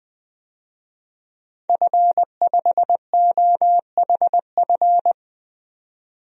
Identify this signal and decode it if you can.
Morse code, heard as a keyed tone, about 20 wpm, 700 Hz. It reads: F5OHF
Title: DXCC-100 Call Signs - 5 Characters - 20wpm